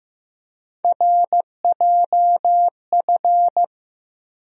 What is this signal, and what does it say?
Morse code, heard as a keyed tone, about 15 wpm, 700 Hz. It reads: RJF
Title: USA Call Sign Postfixes - 3 Letters 15wpm